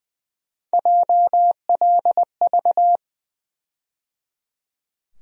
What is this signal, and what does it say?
Morse code, heard as a keyed tone, about 20 wpm, 700 Hz. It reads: JLV
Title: USA Call Sign Postfixes - 3 Letters 20wpm